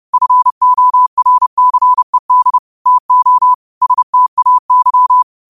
Morse code, heard as a keyed tone, about 30 wpm, 1000 Hz. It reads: FORCED TO STAY
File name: qotdfiles / 30wpm